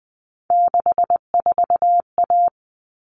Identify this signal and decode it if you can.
Morse code, heard as a keyed tone, about 20 wpm, 700 Hz. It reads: 64A
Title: All Cut Numbers - 3-Digit 20wpm